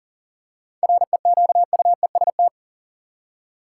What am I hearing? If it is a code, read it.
Morse code, heard as a keyed tone, about 40 wpm, 700 Hz. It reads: REQUEST
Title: Seven Letter Words - 40wpm